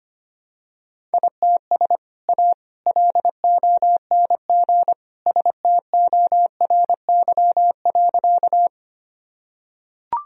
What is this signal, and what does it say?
Morse code, heard as a keyed tone, about 25 wpm, 700 Hz. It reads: ITS A LONG STORY.
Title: Sentences from Top 1000 Words - Review Part 1 - 25wpm